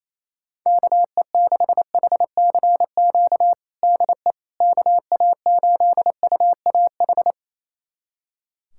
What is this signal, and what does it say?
Morse code, heard as a keyed tone, about 28 wpm, 700 Hz. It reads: KE6HCQ DE KA8UAH
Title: USA Callsign DE Callsign 28wpm